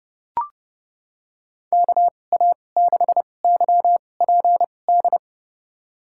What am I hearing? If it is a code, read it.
Morse code, heard as a keyed tone, about 30 wpm, spaced as for 22 wpm, 700 Hz. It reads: KA6YPD
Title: USA Call Signs - 6 Characters - 2x Character Spacing - 30wpm